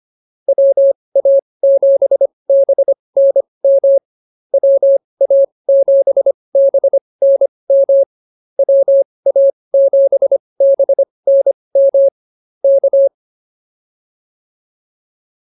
Morse code, heard as a keyed tone, about 24 wpm, spaced as for 20 wpm, 550 Hz. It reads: WA7BNM WA7BNM WA7BNM K
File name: QSO106 20